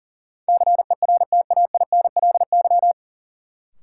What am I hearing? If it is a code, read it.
Morse code, heard as a keyed tone, about 40 wpm, 700 Hz. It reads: CERTAINLY
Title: Sentences from Top 900 Words - New Word Drill - 40wpm